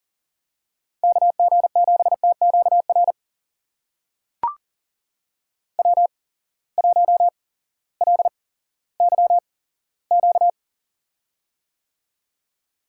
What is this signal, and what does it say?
Morse code, heard as a keyed tone, about 40 wpm, spaced as for 18 wpm, 700 Hz. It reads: KG7TQR W1LYQ
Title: USA Call Signs - 8x Character Spacing - 40wpm